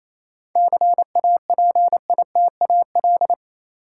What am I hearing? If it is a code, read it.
Morse code, heard as a keyed tone, about 28 wpm, 700 Hz. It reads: CAPITAL
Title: Sentences from Top 1000 Words - New Word Drill - 28wpm